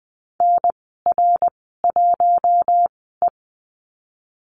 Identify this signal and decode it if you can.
Morse code, heard as a keyed tone, about 20 wpm, spaced as for 15 wpm, 700 Hz. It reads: NR1E
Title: 055 Lesson 10g - Letter r - Call Signs 20wpm